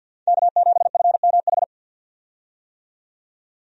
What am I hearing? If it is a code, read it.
Morse code, heard as a keyed tone, about 52 wpm, 700 Hz. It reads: K7FMH
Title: USA Call Signs - 5 Characters - 50wpm